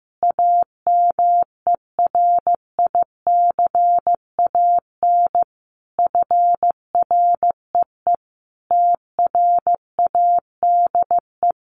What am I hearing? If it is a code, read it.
Morse code, heard as a keyed tone, about 15 wpm, 700 Hz. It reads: AMERICAN FREE TRADE
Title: Sets of 5 Words Encore 15wpm